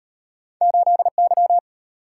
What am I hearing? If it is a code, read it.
Morse code, heard as a keyed tone, about 33 wpm, 690 Hz. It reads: 8Y